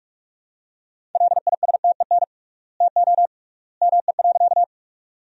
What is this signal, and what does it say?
Morse code, heard as a keyed tone, about 46 wpm, 700 Hz. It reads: LISTEN TO ME.
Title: 3x Word-Spacing - Sentences from Top 400 Words - 45wpm